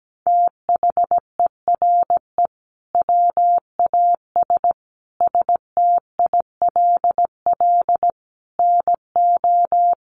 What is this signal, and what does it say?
Morse code, heard as a keyed tone, about 17 wpm, 700 Hz. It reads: THERE WAS STILL NO
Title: Sentences from Top 1400 Words 17wpm